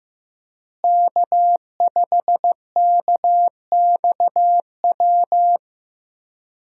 Morse code, heard as a keyed tone, about 15 wpm, 700 Hz. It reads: K5KXW
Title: USA Call Signs - 5 Characters - Repeated 3x - 15wpm